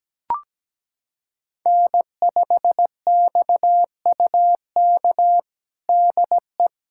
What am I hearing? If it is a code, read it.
Morse code, heard as a keyed tone, about 17 wpm, 700 Hz. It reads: N5XUK DE
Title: USA Callsign DE Callsign - 5 Characters - 17wpm